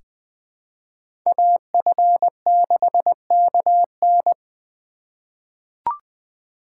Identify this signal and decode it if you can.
Morse code, heard as a keyed tone, about 20 wpm, 700 Hz. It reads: AF6KN
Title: USA Call Signs - 5 Characters - 3x Character Spacing - 20wpm